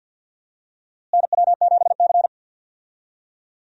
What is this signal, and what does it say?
Morse code, heard as a keyed tone, about 45 wpm, 700 Hz. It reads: NW7C
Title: USA Call Signs - 4 Characters - Repeated 3x - 50wpm